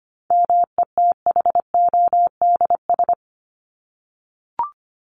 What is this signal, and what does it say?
Morse code, heard as a keyed tone, about 25 wpm, 700 Hz. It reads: METHODS
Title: ES:S Suffix - Top 100 Words with Base Words 25wpm